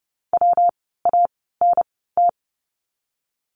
Morse code, heard as a keyed tone, about 30 wpm, spaced as for 17 wpm, 700 Hz. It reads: WANT
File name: course / 30:10wpm / 237 Lesson 38f - Letter q - Words 30wpm